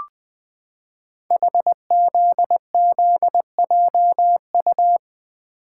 Morse code, heard as a keyed tone, about 20 wpm, 700 Hz. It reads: HZZJU
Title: Code Groups - 5 Letters - 20wpm